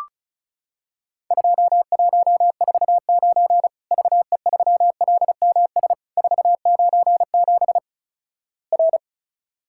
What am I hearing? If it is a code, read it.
Morse code, heard as a keyed tone, about 36 wpm, 700 Hz. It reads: 2149 VE3LMS 497 R
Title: Worked All Europe DX Contest - QTC Exchange - 35wpm